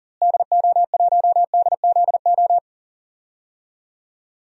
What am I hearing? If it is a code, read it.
Morse code, heard as a keyed tone, about 40 wpm, 700 Hz. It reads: DO1DZO
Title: DXCC-100 Call Signs - 6 Characters - Repeated 2x - 40wpm